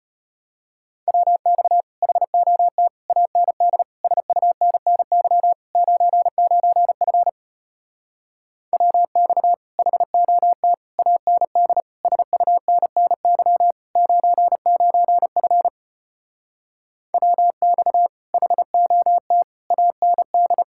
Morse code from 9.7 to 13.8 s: HOT AND SUNNY

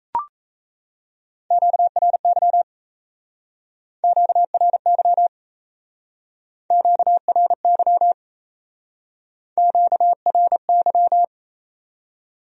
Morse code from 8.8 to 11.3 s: QRY

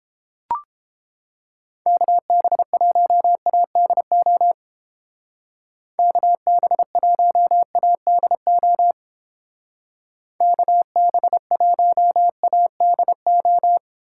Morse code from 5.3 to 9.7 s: KB1ADO

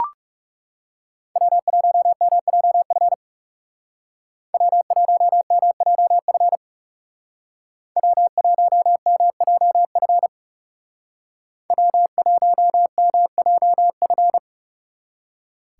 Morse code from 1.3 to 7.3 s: W1MJF W1MJF

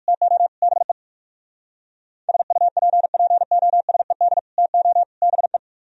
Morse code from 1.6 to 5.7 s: SUPPOSED TO BE

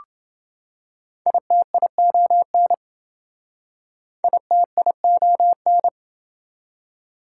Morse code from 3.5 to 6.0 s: ITION

